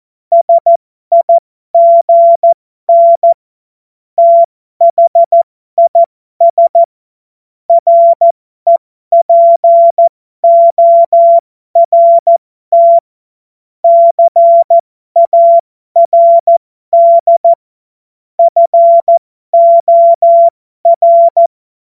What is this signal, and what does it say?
Morse code, heard as a keyed tone, about 14 wpm, spaced as for 12 wpm, 680 Hz. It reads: SIGN THIS REPORT CARD FOR
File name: Morsekurs / Witze